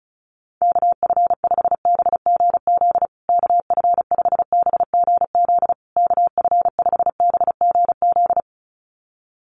Morse code, heard as a keyed tone, about 34 wpm, 700 Hz. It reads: KF5BGZ KF5BGZ KF5BGZ
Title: USA Call Signs Repeated 3x 35wpm